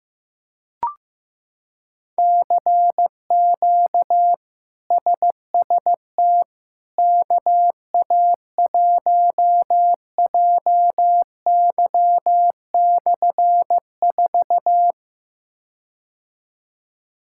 Morse code, heard as a keyed tone, about 15 wpm, 700 Hz. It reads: CQ SST KA1JY/4
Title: K1USN SST QSO Practice - 15wpm